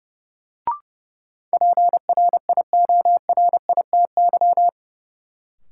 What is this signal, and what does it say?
Morse code, heard as a keyed tone, about 30 wpm, 700 Hz. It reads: PRIORITY
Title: Sentences from Top 1600 Words - New Word Drill - 30wpm